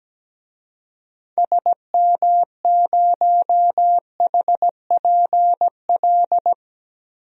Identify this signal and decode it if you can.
Morse code, heard as a keyed tone, about 17 wpm, 700 Hz. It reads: SM0HPL
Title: DXCC-100 Call Signs - 6 Characters - 17wpm